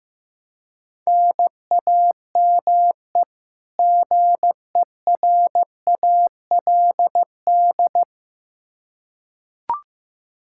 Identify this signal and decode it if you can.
Morse code, heard as a keyed tone, about 15 wpm, 700 Hz. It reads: NAME GERALD
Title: Practice QSO - Exchange 1 - Beginning - 15wpm